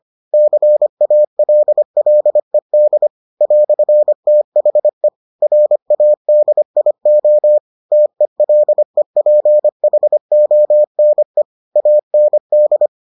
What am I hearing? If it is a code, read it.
Morse code, heard as a keyed tone, about 25 wpm, 600 Hz. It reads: CALLED "THE RADIO TELEPHONE AND